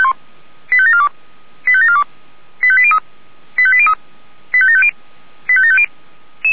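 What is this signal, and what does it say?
Five-tone selective call (ZVEI), heard as quick runs of five tones, tones of about 70 ms. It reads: 85742 85742 85803 85803 85750 85750
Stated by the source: ZVEI five-tone selective call received by SDR.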